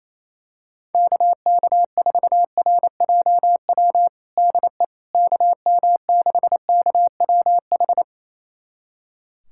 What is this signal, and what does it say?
Morse code, heard as a keyed tone, about 28 wpm, 700 Hz. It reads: KK4RJW DE KM6KWH
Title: USA Callsign DE Callsign 28wpm